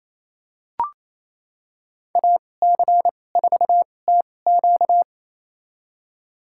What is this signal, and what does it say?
Morse code, heard as a keyed tone, about 28 wpm, spaced as for 20 wpm, 700 Hz. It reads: AC4TQ
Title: USA Call Signs - 5 Characters - 2x Character Spacing - 28wpm